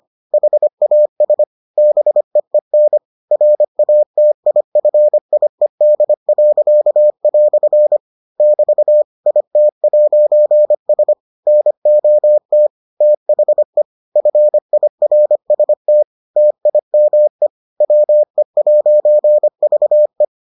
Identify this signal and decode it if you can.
Morse code, heard as a keyed tone, about 25 wpm, 600 Hz. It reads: HAS BEEN RATIFIED." <BT> IT'S NOT THE FIRST TIME WE'VE